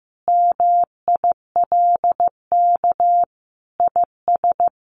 Morse code, heard as a keyed tone, about 14 wpm, 700 Hz. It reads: MILK IS
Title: Additional Easy Phrases 15wpm